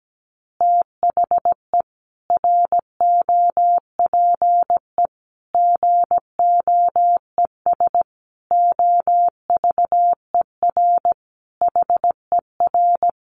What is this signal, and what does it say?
Morse code, heard as a keyed tone, about 17 wpm, 700 Hz. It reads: THE ROPE GOES OVER HER
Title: Easy Phrases 17wpm